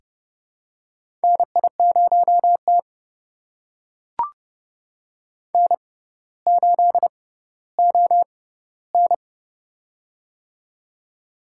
Morse code, heard as a keyed tone, about 30 wpm, spaced as for 15 wpm, 700 Hz. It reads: NI0T N8ON
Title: USA Call Signs - 4 Characters - 6x Character Spacing - 30wpm